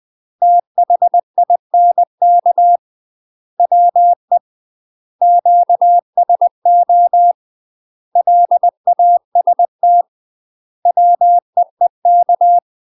Morse code, heard as a keyed tone, about 20 wpm, 700 Hz. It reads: THINK WE QSO LAST WEEK
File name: UK MASTER QSOs TRAINING / 20WPM / UK MASTER QSOs 0014